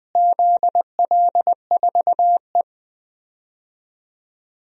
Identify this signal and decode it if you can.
Morse code, heard as a keyed tone, about 20 wpm, 700 Hz. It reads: ZL4E
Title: DXCC-100 Call Signs - 4 Characters - Repeated 2x - 20wpm